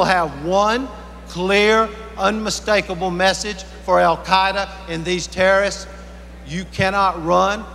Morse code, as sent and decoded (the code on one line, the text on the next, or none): none